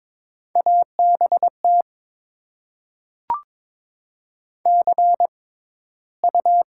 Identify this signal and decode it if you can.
Morse code, heard as a keyed tone, about 22 wpm, spaced as for 14 wpm, 700 Hz. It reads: ABT CU
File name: Top 90 QSO Elements - 6x Character Spacing - 22wpm